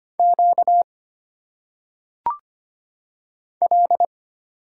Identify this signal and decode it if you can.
Morse code, heard as a keyed tone, about 25 wpm, 700 Hz. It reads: Q L
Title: Single Letters - Phonetic Pronunciation 25wpm